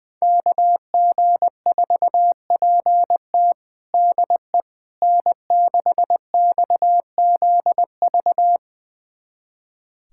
Morse code, heard as a keyed tone, about 20 wpm, 700 Hz. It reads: KG4PT DE N6XZV
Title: USA Callsign DE Callsign - 5 Characters - 20wpm